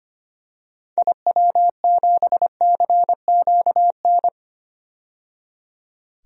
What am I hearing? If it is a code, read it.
Morse code, heard as a keyed tone, about 25 wpm, 700 Hz. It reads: IW7CQN
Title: DXCC-100 Call Signs - 6 Characters - 25wpm